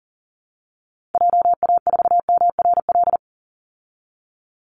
Morse code, heard as a keyed tone, about 40 wpm, 700 Hz. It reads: JA4MRL
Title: Call Signs Repeated 3x - 40wpm